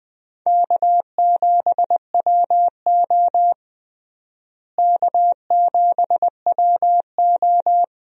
Morse code, heard as a keyed tone, about 20 wpm, 700 Hz. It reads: K7WO K7WO
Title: USA Call Signs - 4 Characters - Repeated 2x - 20wpm